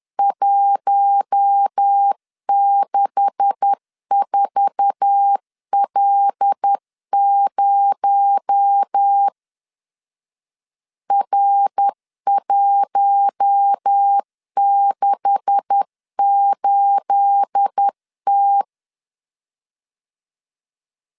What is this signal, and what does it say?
Morse code, heard as a keyed tone, about 10 wpm, 800 Hz. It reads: 164L0 R168T
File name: LEZIONI / 9 286 10wpm 2735